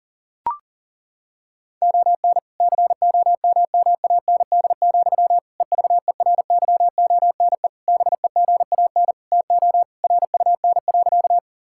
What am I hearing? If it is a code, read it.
Morse code, heard as a keyed tone, about 37 wpm, 700 Hz. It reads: ON COMMAND, EVERYONE BEGAN TO RUN.
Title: Sentences from Top 1600 Words 40wpm